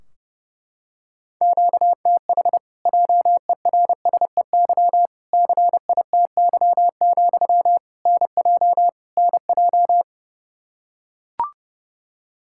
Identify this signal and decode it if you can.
Morse code, heard as a keyed tone, about 30 wpm, 700 Hz. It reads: QTH JERSEY CITY, NJ NJ
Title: Practice QSO - Exchange 1 - Beginning - 30wpm